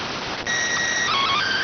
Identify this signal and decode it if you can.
SSTV (slow-scan television), PD 290.